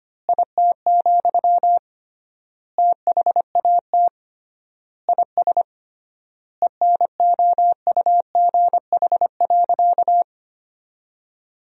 Morse code, heard as a keyed tone, about 25 wpm, 700 Hz. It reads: IT, THAT IS ENOUGH.
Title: 3x Word-Spacing - Sentences from Top 400 Words - 25wpm